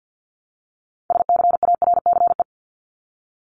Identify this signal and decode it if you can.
Morse code, heard as a keyed tone, about 49 wpm, 700 Hz. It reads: SCARCE
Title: LY Suffix - Top 100 Words with Base Words 50wpm